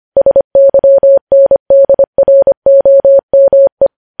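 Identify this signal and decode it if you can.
Morse code, heard as a keyed tone, about 25 wpm, 550 Hz. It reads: SYNDROME
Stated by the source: News headlines sent at 25.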